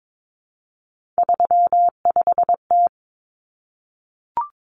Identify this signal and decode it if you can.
Morse code, heard as a keyed tone, about 22 wpm, 700 Hz. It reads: 35T
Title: Common Cut Numbers - 3-Digit 22wpm